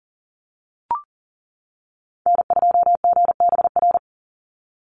Morse code, heard as a keyed tone, about 39 wpm, 700 Hz. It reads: N2GBR
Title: NA SOTA Activator Callsigns 40wpm